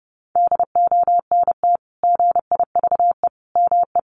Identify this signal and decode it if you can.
Morse code, heard as a keyed tone, about 30 wpm, 700 Hz. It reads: DONT GIVE ME
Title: Sentences from Top 600 Words 30wpm